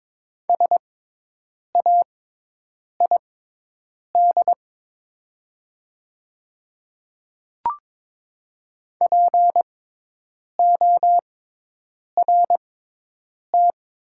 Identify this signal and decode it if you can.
Morse code, heard as a keyed tone, about 22 wpm, spaced as for 8 wpm, 700 Hz. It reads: SAID PORT
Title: 6x Character Spacing - Top 300 Words - 22wpm